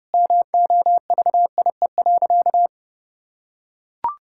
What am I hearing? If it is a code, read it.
Morse code, heard as a keyed tone, about 32 wpm, 700 Hz. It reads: MOVIE.
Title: Sentences from Top 1000 Words - Review Part 5 - 30wpm